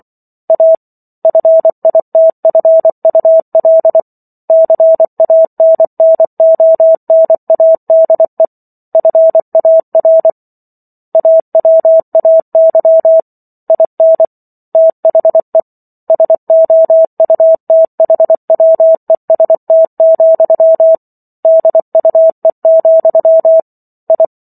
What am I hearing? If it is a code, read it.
Morse code, heard as a keyed tone, about 24 wpm, 650 Hz. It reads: A FITFUL CANNONADE FAR AWAY IN THE SOUTHWEST, DUE, I